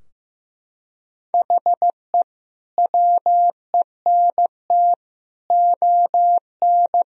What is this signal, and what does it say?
Morse code, heard as a keyed tone, about 15 wpm, 700 Hz. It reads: HE WENT ON
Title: Sentences from Top 1000 Words - Review Part 3 - 15wpm